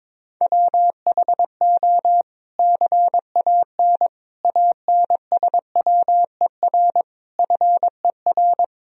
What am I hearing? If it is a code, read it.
Morse code, heard as a keyed tone, about 22 wpm, 700 Hz. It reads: WHO CAN ANSWER FER
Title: Sentences from Top 300 Words and Abbreviations 22wpm